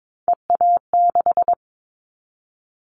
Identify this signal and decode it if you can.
Morse code, heard as a keyed tone, about 22 wpm, 700 Hz. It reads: EA6
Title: Callsign Prefixes for Top 100 DXCC - 22wpm